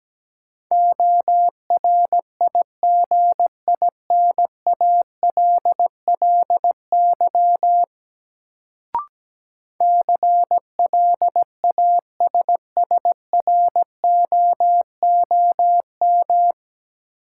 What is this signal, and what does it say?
Morse code, heard as a keyed tone, about 17 wpm, 700 Hz. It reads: ORIGINALLY CLASSROOM
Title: Sentences from Top 1500 Words - New Word Drill - 17wpm